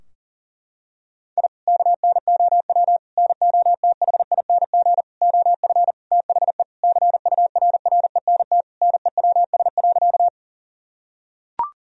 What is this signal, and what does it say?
Morse code, heard as a keyed tone, about 40 wpm, 700 Hz. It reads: I KNOW NOTHING OF THE CURRENT NEWS.
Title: Sentences from Top 1000 Words - Review Part 4 - 40wpm